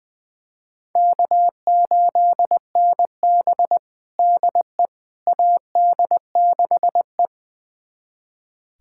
Morse code, heard as a keyed tone, about 20 wpm, 700 Hz. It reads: K8NB DE AD6E
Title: USA Callsign DE Callsign - 4 Characters - 20wpm